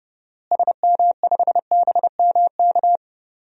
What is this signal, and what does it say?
Morse code, heard as a keyed tone, about 30 wpm, 700 Hz. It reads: SM5BMK
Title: DXCC-100 Call Signs - 6 Characters - Repeated 2x - 30wpm